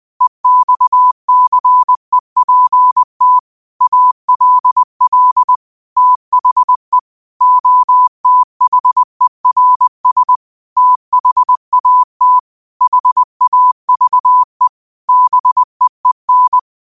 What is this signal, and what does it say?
Morse code, heard as a keyed tone, about 20 wpm, 1000 Hz. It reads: EXCEPT ALL THE OTHERS THAT HAVE BEEN